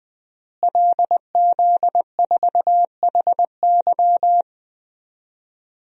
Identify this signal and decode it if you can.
Morse code, heard as a keyed tone, about 20 wpm, 700 Hz. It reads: LZ4HY
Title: DXCC-100 Call Signs - 5 Characters - Repeated 3x - 20wpm